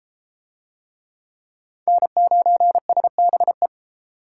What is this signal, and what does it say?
Morse code, heard as a keyed tone, about 30 wpm, 700 Hz. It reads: N9SBE